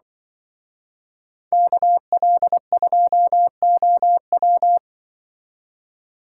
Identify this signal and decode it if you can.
Morse code, heard as a keyed tone, about 24 wpm, 700 Hz. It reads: KL2OW
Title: Speed Racing - USA Call Signs 20wpm